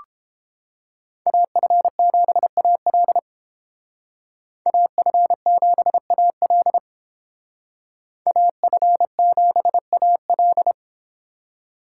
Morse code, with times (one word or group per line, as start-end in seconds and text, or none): 1.3-3.2 AF7AL
4.7-6.8 AF7AL
8.3-10.7 AF7AL